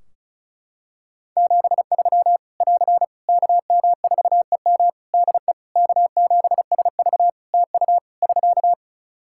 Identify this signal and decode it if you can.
Morse code, heard as a keyed tone, about 36 wpm, 700 Hz. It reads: 73 <AR> KM4EM DE K7SV TU <SK>
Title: Practice QSO - Exchange 4 - Ending - 35wpm